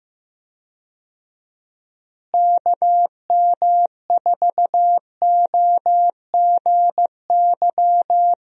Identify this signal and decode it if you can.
Morse code, heard as a keyed tone, about 15 wpm, 700 Hz. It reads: KM4OGY
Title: USA Call Signs - 6 Characters - Repeated 3x - 15wpm